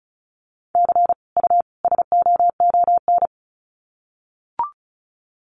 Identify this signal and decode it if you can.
Morse code, heard as a keyed tone, about 34 wpm, 700 Hz. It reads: C U SOON
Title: Common QSO Phrases 35wpm